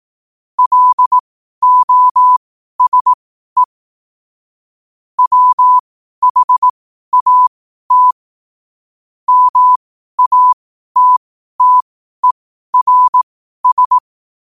Morse code, heard as a keyed tone, about 18 wpm, spaced as for 12 wpm, 1000 Hz. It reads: LOSE WHAT MATTERS
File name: qotdfiles / 12wpm